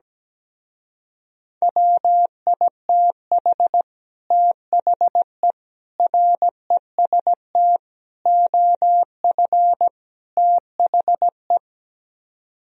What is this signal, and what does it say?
Morse code, heard as a keyed tone, about 17 wpm, 700 Hz. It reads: WITH THE REST OF THE
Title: Speed-Racing - Sets of 5 Words 17wpm